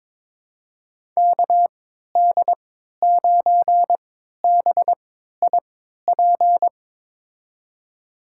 Morse code, heard as a keyed tone, about 22 wpm, spaced as for 13 wpm, 700 Hz. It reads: KD9BIP
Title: USA Call Signs - 6 Characters - 3x Character Spacing - 22wpm